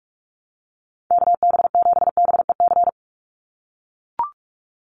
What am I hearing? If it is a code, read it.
Morse code, heard as a keyed tone, about 44 wpm, 700 Hz. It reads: KB7BEC
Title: USA Call Signs 45wpm